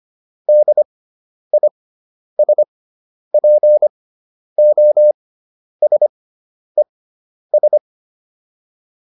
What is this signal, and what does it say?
Morse code, heard as a keyed tone, about 25 wpm, spaced as for 10 wpm, 600 Hz. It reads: DISPOSES